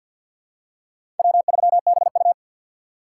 Morse code, heard as a keyed tone, about 52 wpm, 700 Hz. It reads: W3BU
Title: USA Call Signs - 4 Characters - 2x Character Spacing - 50wpm